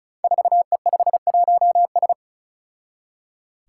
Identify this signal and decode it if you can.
Morse code, heard as a keyed tone, about 36 wpm, 700 Hz. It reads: 4E51S
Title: Five Challenging Letter-Number Combinations 35wpm